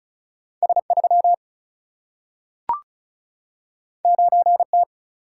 Morse code, heard as a keyed tone, about 36 wpm, 700 Hz. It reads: S3 9T
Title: Single-Letter Single-Number 35wpm